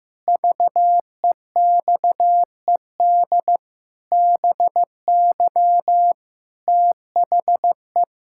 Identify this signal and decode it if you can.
Morse code, heard as a keyed tone, about 15 wpm, 700 Hz. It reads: VEXED BY THE
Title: Pangram Challenge v2 - 15wpm